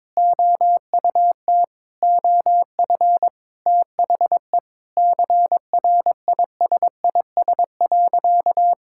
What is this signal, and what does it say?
Morse code, heard as a keyed tone, about 21 wpm, 700 Hz. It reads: OUT OF THE CRISIS.
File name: practice / 22wpm / Sentences with 8-Word Phrases - 22wpm